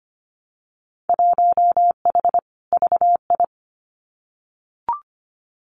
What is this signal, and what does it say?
Morse code, heard as a keyed tone, about 25 wpm, 700 Hz. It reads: 1H VI
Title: Winter Field Day Exchanges Encore 25wpm